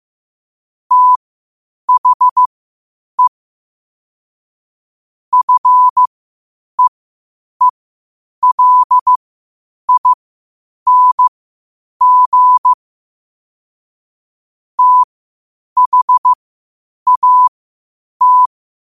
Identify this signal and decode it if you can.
Morse code, heard as a keyed tone, about 15 wpm, spaced as for 8 wpm, 1000 Hz. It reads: THE FEELING THAT